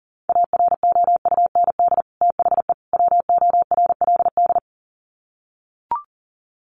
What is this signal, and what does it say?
Morse code, heard as a keyed tone, about 40 wpm, 700 Hz. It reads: AROUND THE WORLD
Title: Sets of 3 Words 40wpm